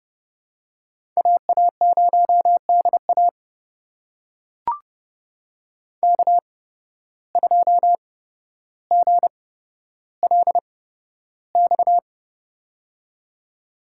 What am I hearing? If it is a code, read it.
Morse code, heard as a keyed tone, about 30 wpm, spaced as for 13 wpm, 700 Hz. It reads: AA0DA K2GLX
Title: USA Call Signs - 5 Characters - 8x Character Spacing - 30wpm